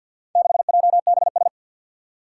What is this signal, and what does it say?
Morse code, heard as a keyed tone, about 49 wpm, 700 Hz. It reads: 6JBS
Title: Four Challenging Letter-Number Combinations 50wpm